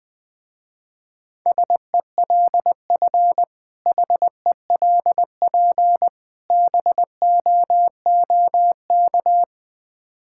Morse code, heard as a keyed tone, about 20 wpm, 700 Hz. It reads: SELF HELP BOOK
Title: Difficult Words with Hints - 20wpm